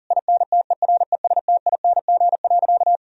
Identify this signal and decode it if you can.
Morse code, heard as a keyed tone, about 40 wpm, 690 Hz. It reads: INTERESTING.
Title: Sentences from Top 1000 Words - Review Part 4 - 40wpm